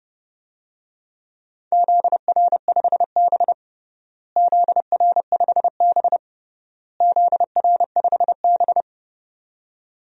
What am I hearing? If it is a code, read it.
Morse code, heard as a keyed tone, about 30 wpm, 700 Hz. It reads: ZR5B ZR5B ZR5B
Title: DXCC-100 Call Signs - 4 Characters - Repeated 3x - 30wpm